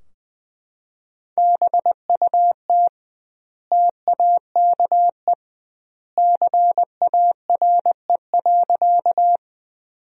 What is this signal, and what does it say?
Morse code, heard as a keyed tone, about 20 wpm, 700 Hz. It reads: BUT TAKE CARE.
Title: 2x Word-Spacing - Sentences from Top 300 Words - 20wpm